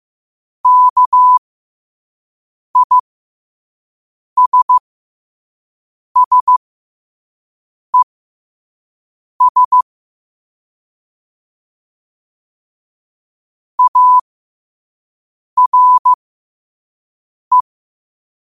Morse code, heard as a keyed tone, about 15 wpm, spaced as for 5 wpm, 1000 Hz. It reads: KISSES ARE
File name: qotdfiles / 5wpm